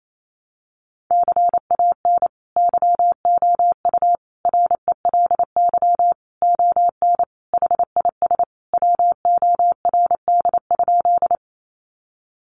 Morse code, heard as a keyed tone, about 28 wpm, 700 Hz. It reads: CAN YOU RELY ON HIS WORD?
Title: Sentences from Top 1200 Words 28wpm